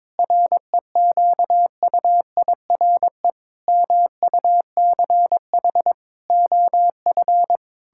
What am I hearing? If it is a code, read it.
Morse code, heard as a keyed tone, about 22 wpm, 700 Hz. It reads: REQUIRE MUCH OF
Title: Sentences from Top 1000 Words - Review Part 6 - 22wpm